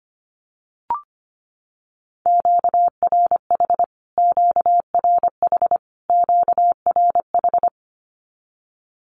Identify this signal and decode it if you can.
Morse code, heard as a keyed tone, about 25 wpm, 700 Hz. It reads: QRH QRH QRH
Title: Q-Codes Repeated 3x 25wpm